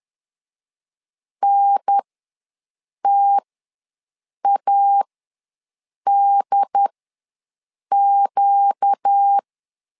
Morse code, heard as a keyed tone, about 10 wpm, spaced as for 6 wpm, 800 Hz. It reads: NTADQ
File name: LEZIONI / 4 Q ws 10wpm 2306